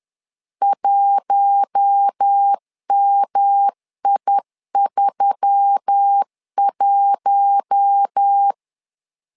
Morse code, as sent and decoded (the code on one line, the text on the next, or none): .---- -- .. ...-- .----
1MI31